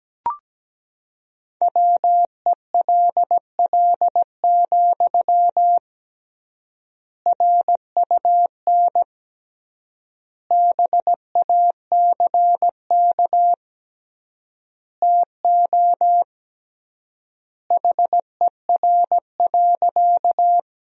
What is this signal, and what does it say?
Morse code, heard as a keyed tone, about 17 wpm, 700 Hz. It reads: WELL, RUN BACK TO HER.
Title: 3x Word-Spacing - Sentences from Top 300 Words - 17wpm